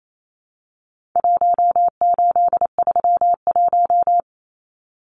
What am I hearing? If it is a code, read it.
Morse code, heard as a keyed tone, about 28 wpm, 700 Hz. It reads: 1831
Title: Numbers - Contest Style - 4-Digit 28wpm